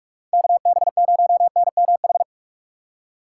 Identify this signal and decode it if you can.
Morse code, heard as a keyed tone, about 54 wpm, 700 Hz. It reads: KB0NMH
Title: Speed Racing - USA 2x3 Call Signs 45wpm